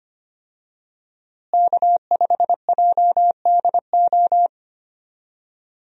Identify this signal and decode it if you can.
Morse code, heard as a keyed tone, about 25 wpm, 700 Hz. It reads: K5JDO
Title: USA Call Signs - 5 Characters - Repeated 3x - 25wpm